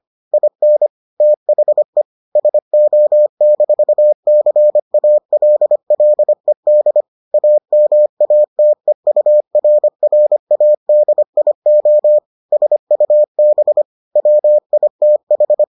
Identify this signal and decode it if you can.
Morse code, heard as a keyed tone, about 25 wpm, 600 Hz. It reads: IN THE SO-CALLED AMATEURRADIO SUB WITH